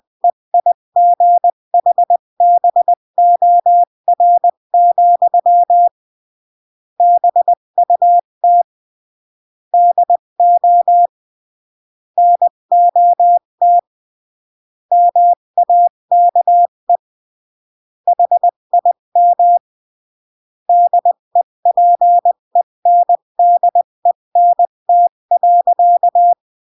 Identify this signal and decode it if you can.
Morse code, heard as a keyed tone, about 20 wpm, 700 Hz. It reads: EIGHBOR, BUT DO NOT MAKE HIM DEPENDENT.
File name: CW / jodorowsky / 82 commandments 20-15